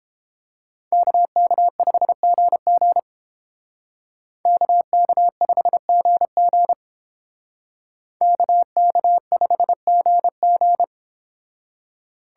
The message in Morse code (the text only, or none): KK5GG KK5GG KK5GG